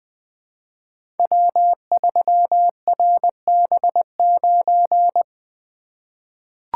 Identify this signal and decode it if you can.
Morse code, heard as a keyed tone, about 20 wpm, 700 Hz. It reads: W3RB9
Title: Code Groups - 5 Letter:Numbers - 20wpm